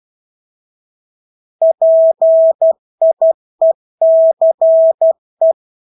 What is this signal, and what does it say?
Morse code, heard as a keyed tone, about 12 wpm, 650 Hz. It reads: PIECE
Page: Engelske ord 12 wpm